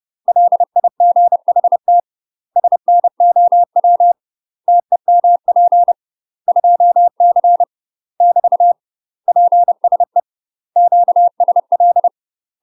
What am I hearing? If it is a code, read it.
Morse code, heard as a keyed tone, about 30 wpm, 700 Hz. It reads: LIGHT SNOW TEMP 2C <BT> PSE QSL
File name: UK MASTER QSOs TRAINING / 30WPM / UK MASTER QSOs 0011